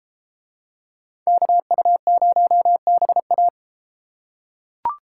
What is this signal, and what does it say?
Morse code, heard as a keyed tone, about 33 wpm, 700 Hz. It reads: KU0BA